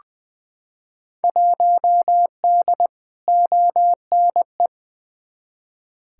Morse code, heard as a keyed tone, about 20 wpm, 700 Hz. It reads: 1D ONE
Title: Field Day Exchanges - 20wpm